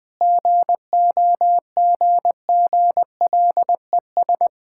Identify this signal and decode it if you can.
Morse code, heard as a keyed tone, about 21 wpm, 700 Hz. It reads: GOGGLES
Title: Difficult Words with Hints - 20wpm